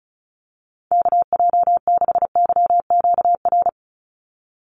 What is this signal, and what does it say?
Morse code, heard as a keyed tone, about 34 wpm, 700 Hz. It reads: KJ6YQR
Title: USA Call Signs 35wpm